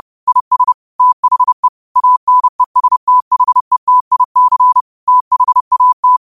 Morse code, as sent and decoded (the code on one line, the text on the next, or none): .. ... / - .... / .- -. ... - .... - .. -.-. / - .... .- -
IS THE ANESTHETIC THAT